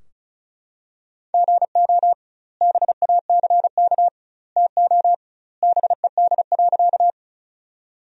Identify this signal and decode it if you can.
Morse code, heard as a keyed tone, about 36 wpm, 700 Hz. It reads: GO BACK TO BED.
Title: 2x Word-Spacing - Sentences from Top 500 Words - 35wpm